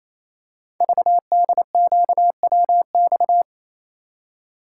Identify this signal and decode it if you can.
Morse code, heard as a keyed tone, about 27 wpm, 700 Hz. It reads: VDQWX
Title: Code Groups - 5 Letter:Numbers - 28wpm